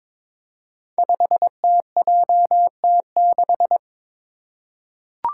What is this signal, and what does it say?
Morse code, heard as a keyed tone, about 22 wpm, 700 Hz. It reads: 5TJT6
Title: Five Challenging Letter-Number Combinations 22wpm